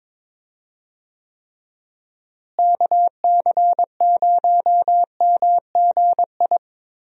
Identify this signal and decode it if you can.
Morse code, heard as a keyed tone, about 22 wpm, 700 Hz. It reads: KC0MGI
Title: USA Call Signs - 6 Characters - Repeated 3x - 22wpm